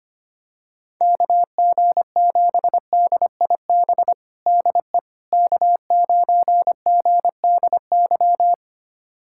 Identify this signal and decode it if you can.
Morse code, heard as a keyed tone, about 25 wpm, 700 Hz. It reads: KG7DIB DE K9GDY
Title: USA Callsign DE Callsign 25wpm